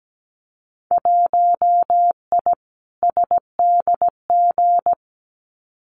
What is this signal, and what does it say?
Morse code, heard as a keyed tone, about 17 wpm, 700 Hz. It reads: 1I SDG
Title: Winter Field Day Exchanges 17wpm